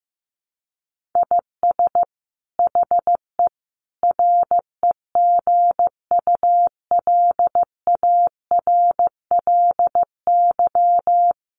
Morse code, heard as a keyed tone, about 14 wpm, 700 Hz. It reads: IS HE REGULARLY
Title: Sentences from Top 1400 Words 15wpm